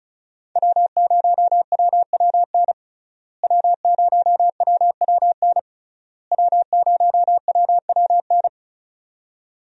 Morse code, heard as a keyed tone, about 36 wpm, 700 Hz. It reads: W0WWN W0WWN W0WWN
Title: USA Call Signs - 5 Characters - Repeated 3x - 35wpm